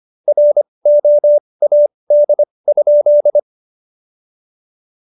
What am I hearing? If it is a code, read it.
Morse code, heard as a keyed tone, about 25 wpm, spaced as for 20 wpm, 590 Hz. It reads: ROAD?
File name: Jokes-25-char-20-eff